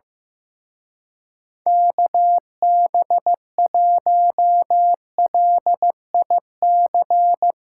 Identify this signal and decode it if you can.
Morse code, heard as a keyed tone, about 16 wpm, 700 Hz. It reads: KB1LIC
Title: Speed Racing - USA 2x3 Call Signs 15wpm